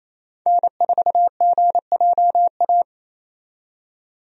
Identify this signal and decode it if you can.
Morse code, heard as a keyed tone, about 28 wpm, 700 Hz. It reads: N4GJA
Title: USA Call Signs - 5 Characters - 28wpm